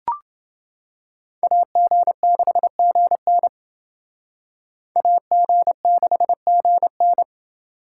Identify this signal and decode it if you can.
Morse code, heard as a keyed tone, about 28 wpm, 700 Hz. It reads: AG6GN AG6GN